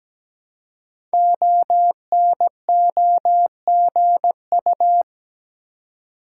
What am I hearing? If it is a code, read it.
Morse code, heard as a keyed tone, about 17 wpm, 700 Hz. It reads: ONOGU
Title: Code Groups - 5 Letters - 17wpm